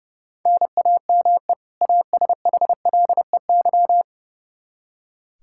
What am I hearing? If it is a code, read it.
Morse code, heard as a keyed tone, about 29 wpm, 700 Hz. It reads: NAME ASHLEY
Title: Practice QSO - Exchange 1 - Beginning - 30wpm